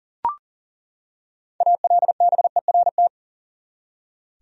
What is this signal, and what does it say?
Morse code, heard as a keyed tone, about 40 wpm, 700 Hz. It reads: ALBERT
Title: USA Common Names - 40wpm